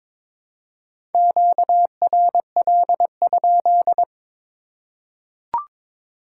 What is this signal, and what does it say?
Morse code, heard as a keyed tone, about 22 wpm, 700 Hz. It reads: QRL?
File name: Top 90 QSO Elements - 8x Character Spacing - 22wpm